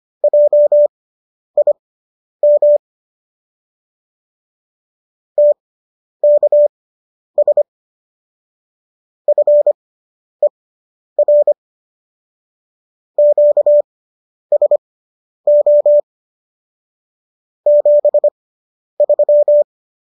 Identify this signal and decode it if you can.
Morse code, heard as a keyed tone, about 25 wpm, spaced as for 10 wpm, 590 Hz. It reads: JIM TKS FER QSO 73